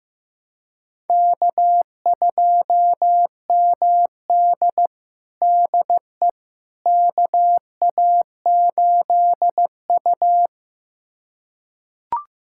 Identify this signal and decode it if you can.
Morse code, heard as a keyed tone, about 15 wpm, 700 Hz. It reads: K2MD DE KA8U
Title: USA Callsign DE Callsign - 4 Characters - 15wpm